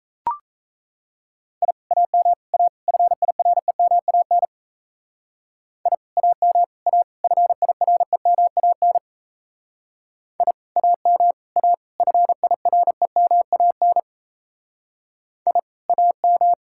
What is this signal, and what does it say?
Morse code, heard as a keyed tone, about 33 wpm, 700 Hz. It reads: I AM A FIREMAN I AM A FIREMAN I AM A FIREMAN I AM
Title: Speed-Racing Common QSO Phrases - 28wpm